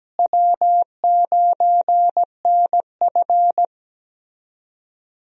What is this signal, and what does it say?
Morse code, heard as a keyed tone, about 17 wpm, 700 Hz. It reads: W9NF
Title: USA Call Signs - 4 Characters - Repeated 2x - 17wpm